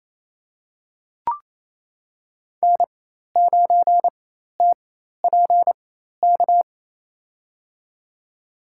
Morse code, heard as a keyed tone, about 28 wpm, spaced as for 13 wpm, 700 Hz. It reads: N9TPK
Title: USA Call Signs - 5 Characters - 4x Character Spacing - 28wpm